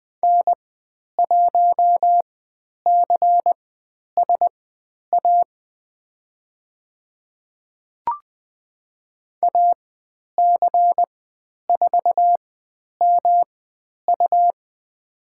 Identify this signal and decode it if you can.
Morse code, heard as a keyed tone, about 20 wpm, spaced as for 10 wpm, 700 Hz. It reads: N1CSA AC4MU
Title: 109 Lesson 18g - Letter m - Call Signs 20wpm